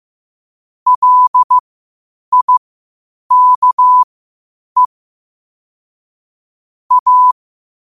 Morse code, heard as a keyed tone, about 15 wpm, spaced as for 8 wpm, 1000 Hz. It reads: LIKE A
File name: 8wpm